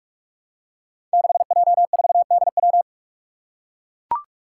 Morse code, heard as a keyed tone, about 44 wpm, 700 Hz. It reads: 6J4DW
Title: Five Challenging Letter-Number Combinations 45wpm